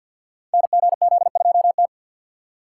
Morse code, heard as a keyed tone, about 48 wpm, 700 Hz. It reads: NGZ2T
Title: Code Groups - 5 Letter:Numbers - 50wpm